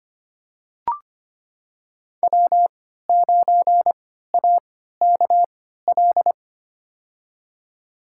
Morse code, heard as a keyed tone, about 25 wpm, spaced as for 14 wpm, 700 Hz. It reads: W9AKL
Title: USA Call Signs - 5 Characters - 3x Character Spacing - 25wpm